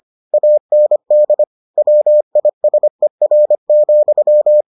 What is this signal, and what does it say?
Morse code, heard as a keyed tone, about 25 wpm, 600 Hz. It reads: AND WISER,